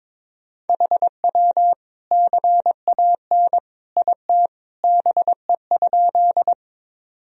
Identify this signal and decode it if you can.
Morse code, heard as a keyed tone, about 22 wpm, 700 Hz. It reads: HW CAN IT BE?
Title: Sentences from Top 100 Words and Abbreviations 22wpm